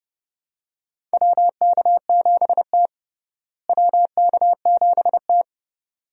Morse code, heard as a keyed tone, about 30 wpm, 700 Hz. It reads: WK7T WK7T
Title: USA Call Signs - 4 Characters - Repeated 2x - 30wpm